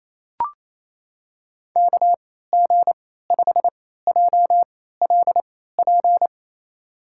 Morse code, heard as a keyed tone, about 29 wpm, spaced as for 16 wpm, 700 Hz. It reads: KG5JLP
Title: USA Call Signs - 3x Character Spacing - 28wpm